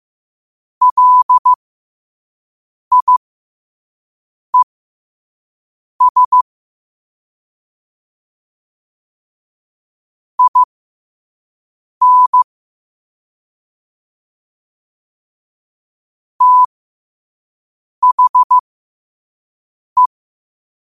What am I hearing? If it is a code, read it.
Morse code, heard as a keyed tone, about 15 wpm, spaced as for 5 wpm, 1000 Hz. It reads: LIES IN THE
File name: qotdfiles / 5wpm